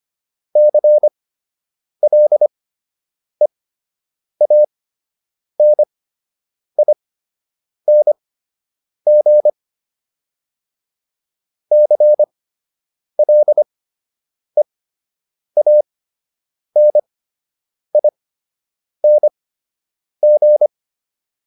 Morse code, heard as a keyed tone, about 25 wpm, spaced as for 8 wpm, 600 Hz. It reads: CLEANING CLEANING
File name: ING8 08